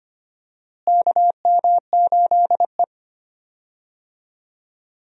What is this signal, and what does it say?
Morse code, heard as a keyed tone, about 25 wpm, 700 Hz. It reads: KM8E
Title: USA Call Signs - 4 Characters - Repeated 2x - 25wpm